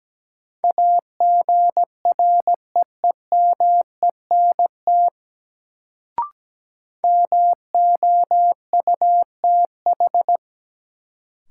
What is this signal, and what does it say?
Morse code, heard as a keyed tone, about 17 wpm, 700 Hz. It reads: AGREEMENT MOUTH
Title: Sentences from Top 1500 Words - New Word Drill - 17wpm